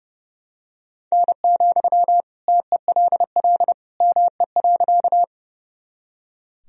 Morse code, heard as a keyed tone, about 30 wpm, 700 Hz. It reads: N, TELL ME.
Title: Sentences from Top 200 Words and Abbreviations 30wpm